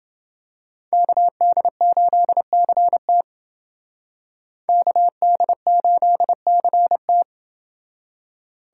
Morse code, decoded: KD8CT KD8CT

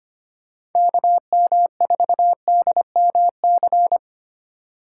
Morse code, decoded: KM4DMC

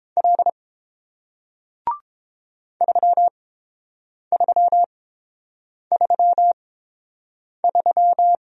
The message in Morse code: L 3333

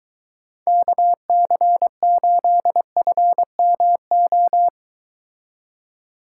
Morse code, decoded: KC8FMO

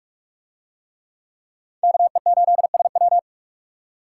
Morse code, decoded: KE8SW

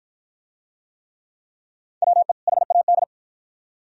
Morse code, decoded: WE HAD

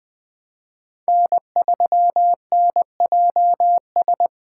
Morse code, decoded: N3NJS

